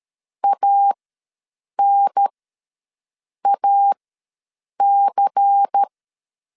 Morse code, heard as a keyed tone, about 13 wpm, spaced as for 7 wpm, 800 Hz. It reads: ANAC
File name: LEZIONI / 6 PKX0 / 6 P ws 12wpm 3117